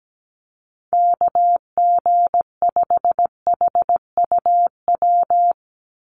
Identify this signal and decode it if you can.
Morse code, heard as a keyed tone, about 17 wpm, 700 Hz. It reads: KG5HUW
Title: USA Call Signs 17wpm